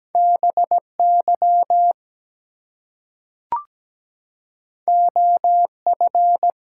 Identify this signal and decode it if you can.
Morse code, heard as a keyed tone, about 17 wpm, 700 Hz. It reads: BY OF
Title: Two Letter Words - 17wpm